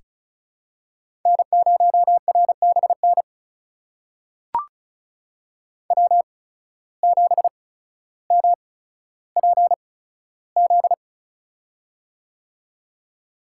Morse code, heard as a keyed tone, about 36 wpm, spaced as for 15 wpm, 700 Hz. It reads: N0RBN W7MPZ
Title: USA Call Signs - 5 Characters - 8x Character Spacing - 35wpm